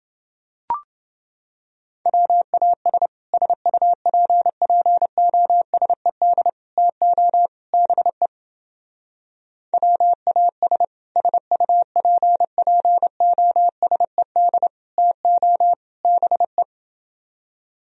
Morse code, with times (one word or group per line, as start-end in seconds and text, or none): 2.1-3.1 WAS
3.3-6.5 SUPPOSED
6.8-7.5 TO
7.7-8.3 BE
9.7-10.8 WAS
11.2-14.7 SUPPOSED
15.0-15.7 TO
16.0-16.6 BE